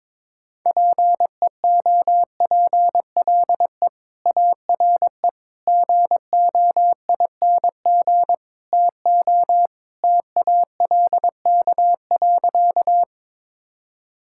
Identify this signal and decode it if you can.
Morse code, heard as a keyed tone, about 21 wpm, 700 Hz. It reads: PEOPLE ARE GOING TO TALK.
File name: practice / 22wpm / Sentences with 4-Word Phrases - 22wpm